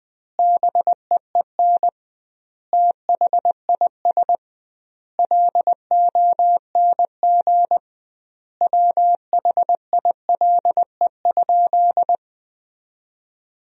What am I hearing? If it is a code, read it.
Morse code, heard as a keyed tone, about 20 wpm, 700 Hz. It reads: BEEN THIS LONG WHILE?
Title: 2x Word-Spacing - Sentences from Top 300 Words - 20wpm